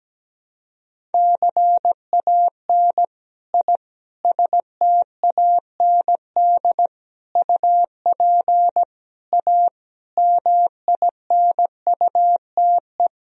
Morse code, heard as a keyed tone, about 17 wpm, 700 Hz. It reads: CAN I STAND UP A MINUTE
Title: Sentences from Top 1000 Words - Review Part 1 - 17wpm